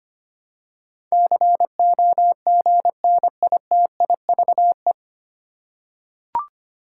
Morse code, heard as a keyed tone, about 25 wpm, 700 Hz. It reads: COGNITIVE
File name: Nine Letter Words - 25wpm